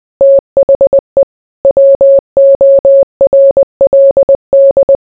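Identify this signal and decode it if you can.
Morse code, heard as a keyed tone, about 20 wpm, 550 Hz. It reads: THE WORLD
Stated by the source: Morse code mixed with noise and music as background sound for sleep.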